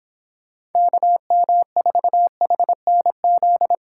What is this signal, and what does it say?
Morse code, heard as a keyed tone, about 26 wpm, 700 Hz. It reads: KM4HNZ